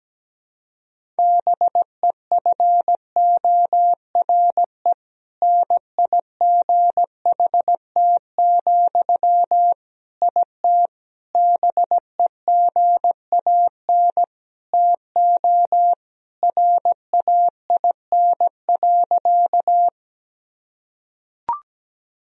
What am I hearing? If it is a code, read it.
Morse code, heard as a keyed tone, about 17 wpm, 700 Hz. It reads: BEFORE NIGHT, IT BEGAN TO RAIN.
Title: Sentences from Top 1000 Words - Review Part 1 - 17wpm